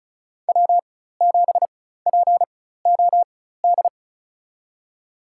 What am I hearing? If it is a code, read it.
Morse code, heard as a keyed tone, about 36 wpm, spaced as for 16 wpm, 700 Hz. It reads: W7POD
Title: USA Call Signs - 5 Characters - 4x Character Spacing - 35wpm